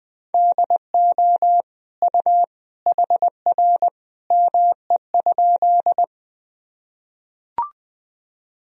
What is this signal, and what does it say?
Morse code, heard as a keyed tone, about 20 wpm, 700 Hz. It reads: DO U HR ME?
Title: Sentences from Top 300 Words and Abbreviations 20wpm